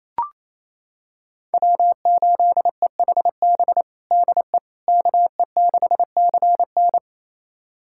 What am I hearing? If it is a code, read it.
Morse code, heard as a keyed tone, about 28 wpm, 700 Hz. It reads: W8EHB DE KE6CN
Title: USA Callsign DE Callsign - 5 Characters - 28wpm